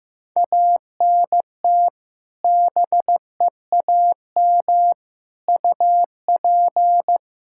Morse code, heard as a keyed tone, about 16 wpm, 700 Hz. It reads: ANT BEAM UP
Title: Practice QSO - Exchange 2 - Info - 15wpm